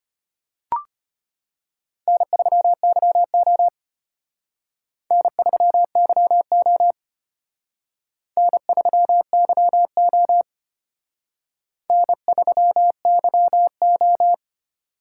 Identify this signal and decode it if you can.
Morse code, heard as a keyed tone, about 31 wpm, 700 Hz. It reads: N3YO N3YO N3YO N3YO
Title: Speed Racing - USA 1x2 Call Signs 25wpm